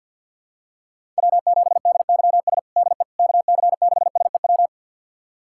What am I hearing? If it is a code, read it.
Morse code, heard as a keyed tone, about 47 wpm, 700 Hz. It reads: W7DYS DE KC6SEW
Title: USA Callsign DE Callsign 50wpm